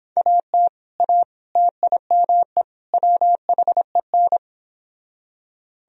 Morse code, heard as a keyed tone, about 29 wpm, 700 Hz. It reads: AT A TIME WHEN